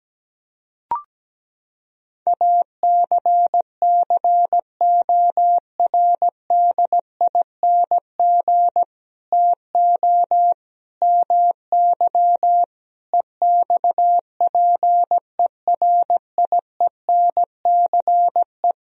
Morse code, heard as a keyed tone, about 17 wpm, 700 Hz. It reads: ACCORDING TO MY EXPERIENCE
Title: Sentences from Top 1000 Words - Review Part 4 - 17wpm